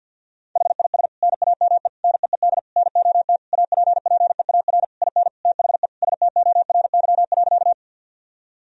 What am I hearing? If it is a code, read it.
Morse code, heard as a keyed tone, about 50 wpm, 700 Hz. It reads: HIS NAME NEED NOT APPEAR IN THE STORY.
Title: Sentences from Top 1000 Words - Review Part 2 - 50wpm